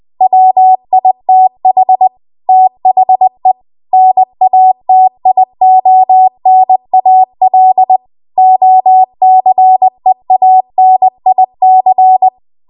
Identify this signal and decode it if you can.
Morse code, heard as a keyed tone, about 20 wpm, 750 Hz. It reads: WITH THE NATIONAL OCEANIC